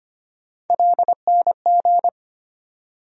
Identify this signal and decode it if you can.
Morse code, heard as a keyed tone, about 25 wpm, 700 Hz. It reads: LNG
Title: USA Call Sign Postfixes - 3 Letters 25wpm